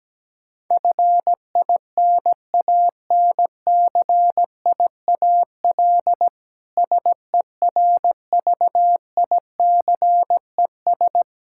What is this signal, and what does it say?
Morse code, heard as a keyed tone, about 17 wpm, 700 Hz. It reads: FINANCIAL SERVICES
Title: K1USN SST QSO Practice 17wpm